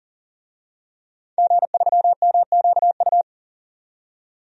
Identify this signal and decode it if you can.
Morse code, heard as a keyed tone, about 37 wpm, 700 Hz. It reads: G3MQU
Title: DXCC-100 Call Signs - 5 Characters - 40wpm